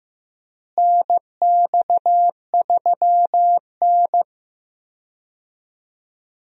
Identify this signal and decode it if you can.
Morse code, heard as a keyed tone, about 15 wpm, 700 Hz. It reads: NX3N
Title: USA Call Signs - 4 Characters - Repeated 2x - 15wpm